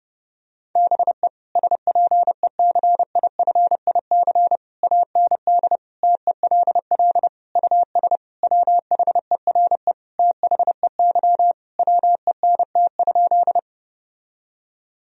Morse code, heard as a keyed tone, about 30 wpm, 700 Hz. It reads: BE SPECIFIC AND TELL US WHERE THEY WENT?
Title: Sentences from Top 1000 Words - Review Part 3 - 30wpm